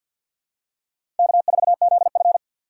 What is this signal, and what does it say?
Morse code, heard as a keyed tone, about 53 wpm, 700 Hz. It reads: K4ZF
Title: USA Call Signs - 4 Characters - 2x Character Spacing - 50wpm